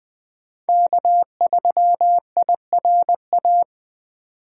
Morse code, heard as a keyed tone, about 20 wpm, 700 Hz. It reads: K3IRA
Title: USA Call Signs - 8x Character Spacing - 20wpm